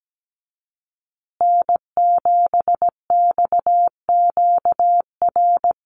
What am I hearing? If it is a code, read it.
Morse code, heard as a keyed tone, about 17 wpm, 700 Hz. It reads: N7XQR
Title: USA Call Signs Repeated 2x 17wpm